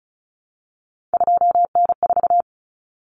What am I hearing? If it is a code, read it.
Morse code, heard as a keyed tone, about 34 wpm, 700 Hz. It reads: 2N4
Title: Common Cut Numbers - 3-Digit 35wpm